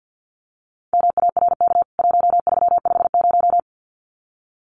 Morse code, heard as a keyed tone, about 50 wpm, 700 Hz. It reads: MARK 1350
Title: CWT Exchange 50wpm